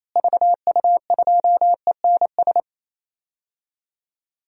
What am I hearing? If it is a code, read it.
Morse code, heard as a keyed tone, about 28 wpm, 700 Hz. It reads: VU2ENS
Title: DXCC-100 Call Signs - 6 Characters - Repeated 3x - 28wpm